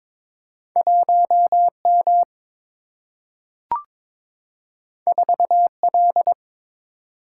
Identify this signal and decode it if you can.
Morse code, heard as a keyed tone, about 22 wpm, 700 Hz. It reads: 1M 4L
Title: Single-Letter Single-Number 22wpm